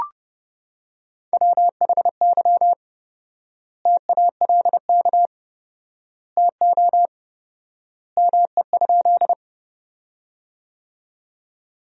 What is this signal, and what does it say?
Morse code, heard as a keyed tone, about 30 wpm, 700 Hz. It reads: WHY TALK TO ME?
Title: 4x Word-Spacing - Sentences from Top 400 Words - 30wpm